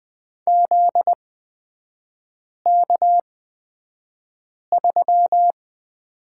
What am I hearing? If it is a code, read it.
Morse code, heard as a keyed tone, about 20 wpm, spaced as for 5 wpm, 700 Hz. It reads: ZK3
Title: 243 Lesson 39e - Letter z - Three Characters 20wpm